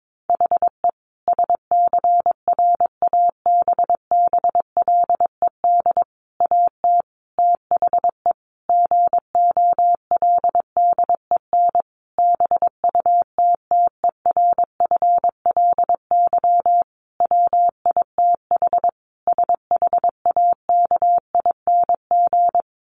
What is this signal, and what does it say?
Morse code, heard as a keyed tone, about 22 wpm, 700 Hz. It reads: HE SCRABBLED AT THE GOLDEN BUTTERFLY WITH SHAKING
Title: A Sound Of Thunder 22wpm